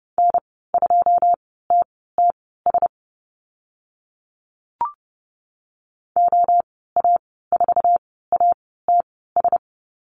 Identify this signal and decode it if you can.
Morse code, heard as a keyed tone, about 32 wpm, spaced as for 17 wpm, 700 Hz. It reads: N2TTS OA4ATS
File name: Lesson 15g - Number 5 - Call Signs 30wpm